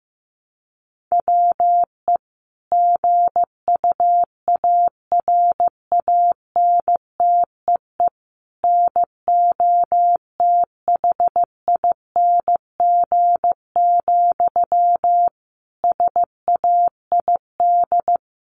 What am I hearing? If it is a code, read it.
Morse code, heard as a keyed tone, about 15 wpm, 700 Hz. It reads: WE GUARANTEE NOTHING, SAID
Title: A Sound Of Thunder 15wpm